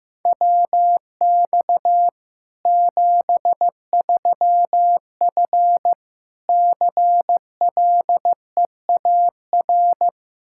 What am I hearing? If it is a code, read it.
Morse code, heard as a keyed tone, about 16 wpm, 700 Hz. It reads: WX 73F CLEAR
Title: Practice QSO - Exchange 2 - Info - 15wpm